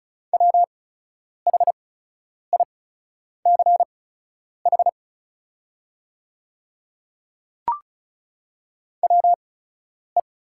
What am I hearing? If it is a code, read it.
Morse code, heard as a keyed tone, about 36 wpm, spaced as for 10 wpm, 690 Hz. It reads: WHICH WE
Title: 8x Character Spacing - Top 100 Words - 35wpm